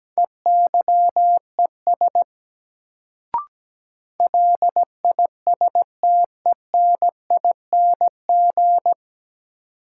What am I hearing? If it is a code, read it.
Morse code, heard as a keyed tone, about 17 wpm, 700 Hz. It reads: EYES LISTENING
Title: Sets of 5 Words - New Word Drill - 17wpm